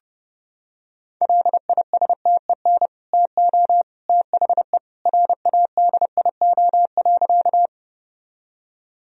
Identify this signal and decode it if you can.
Morse code, heard as a keyed tone, about 30 wpm, 700 Hz. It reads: LISTEN TO THE RADIO.
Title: Sentences from Top 1000 Words - Review Part 3 - 30wpm